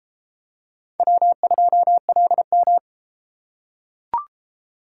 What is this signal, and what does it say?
Morse code, heard as a keyed tone, about 32 wpm, 700 Hz. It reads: W2LM